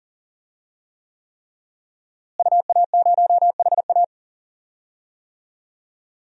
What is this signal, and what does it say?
Morse code, heard as a keyed tone, about 40 wpm, 700 Hz. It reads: UA0HA
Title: DXCC-100 Call Signs - 5 Characters - Repeated 3x - 40wpm